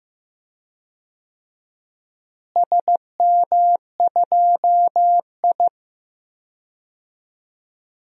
Morse code, heard as a keyed tone, about 15 wpm, 700 Hz. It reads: SM2I
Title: DXCC-100 Call Signs - 4 Characters - Repeated 3x - 15wpm